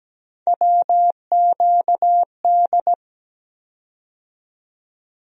Morse code, heard as a keyed tone, about 17 wpm, 700 Hz. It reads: WQD
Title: USA Call Sign Postfixes - 3 Letters 17wpm